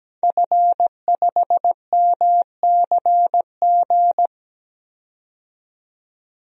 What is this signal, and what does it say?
Morse code, heard as a keyed tone, about 17 wpm, 700 Hz. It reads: F5MCG
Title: DXCC-100 Call Signs - 5 Characters - 17wpm